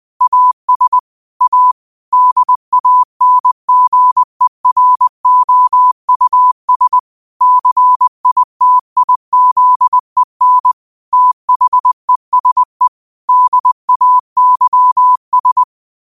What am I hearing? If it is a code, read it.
Morse code, heard as a keyed tone, about 20 wpm, 1000 Hz. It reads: AS A DANGEROUS CITIZEN THESE DAYS
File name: qotdfiles / 20wpm